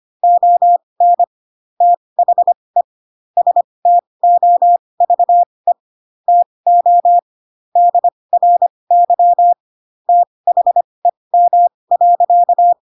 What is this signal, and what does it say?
Morse code, heard as a keyed tone, about 25 wpm, spaced as for 20 wpm, 700 Hz. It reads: ON THE STOVE TO DRY THEM.
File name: Pinocchio / Pinocchio ch6 15 25 ik1hge